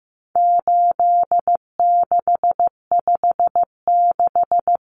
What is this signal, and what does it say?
Morse code, heard as a keyed tone, about 15 wpm, 700 Hz. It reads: 8656
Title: Common Cut Numbers - 4-Digit 15wpm